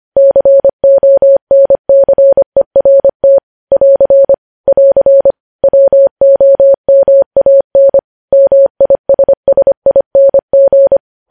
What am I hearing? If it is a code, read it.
Morse code, heard as a keyed tone, about 25 wpm, 550 Hz. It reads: CONCERT <AR> <AR> WOMAN MISSING
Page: News Headlines in Morse Code at 25 WPM